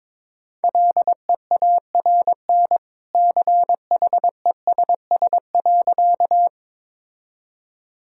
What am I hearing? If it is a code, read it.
Morse code, heard as a keyed tone, about 22 wpm, 700 Hz. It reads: LEARN CHESS.
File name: practice / 22wpm / Easy Sentences with Hints - 22wpm